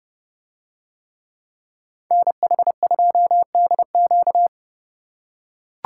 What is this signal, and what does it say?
Morse code, heard as a keyed tone, about 29 wpm, 700 Hz. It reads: NH2DQ